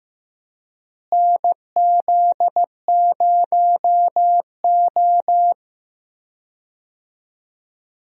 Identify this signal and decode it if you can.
Morse code, heard as a keyed tone, about 15 wpm, 700 Hz. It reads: NZ0O
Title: USA Call Signs - 4 Characters - Repeated 2x - 15wpm